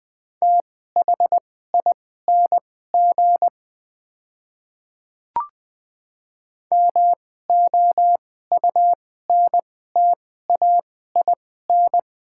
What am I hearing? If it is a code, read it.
Morse code, heard as a keyed tone, about 20 wpm, spaced as for 15 wpm, 700 Hz. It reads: THING MOUNTAIN
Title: 2x Character Spacing - Top 300 Words - 20wpm